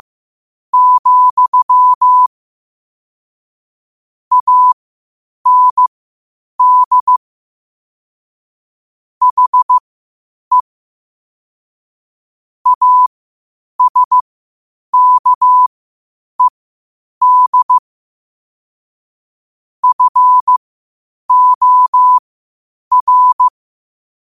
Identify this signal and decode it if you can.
Morse code, heard as a keyed tone, about 15 wpm, spaced as for 8 wpm, 1000 Hz. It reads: , AND HE ASKED FOR
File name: qotdfiles / 8wpm